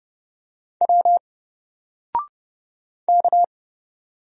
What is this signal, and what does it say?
Morse code, heard as a keyed tone, about 30 wpm, 700 Hz. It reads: W K
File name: Single-Letter-Number-30wpm